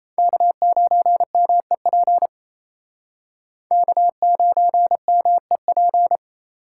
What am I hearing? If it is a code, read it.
Morse code, heard as a keyed tone, about 38 wpm, 700 Hz. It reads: K9MEP K9MEP